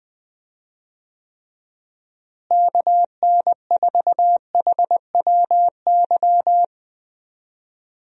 Morse code, heard as a keyed tone, about 20 wpm, 700 Hz. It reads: KN4HWY
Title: USA Call Signs - 6 Characters - Repeated 3x - 20wpm